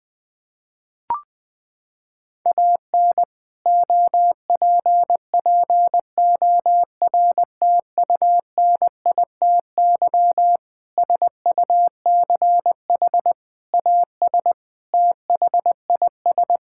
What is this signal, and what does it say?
Morse code, heard as a keyed tone, about 20 wpm, 700 Hz. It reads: AN OPPORTUNITY SUCH AS THIS
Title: Sentences from Top 1600 Words 20wpm